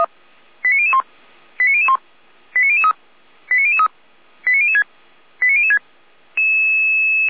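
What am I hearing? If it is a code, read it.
Five-tone selective call (ZVEI), heard as quick runs of five tones, tones of about 70 ms. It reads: 89001 89001 89003 89003 89006 89006
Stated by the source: SDR-received ZVEI five-tone selective call, address 89099, from feuerwehr-orlamuende fire brigade (F85360).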